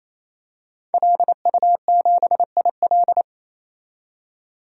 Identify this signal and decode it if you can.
Morse code, heard as a keyed tone, about 28 wpm, 700 Hz. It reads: LU7IL
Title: DXCC-100 Call Signs - 5 Characters - 28wpm